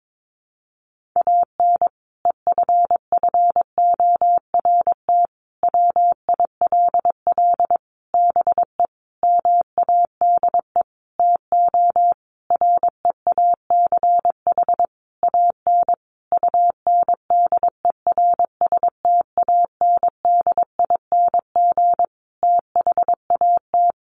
Morse code, heard as a keyed tone, about 22 wpm, 700 Hz. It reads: AN EFFORT WILL BE MADE TO REACH AN UNDERSTANDING THAT
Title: Sentences from Top 1500 Words 22wpm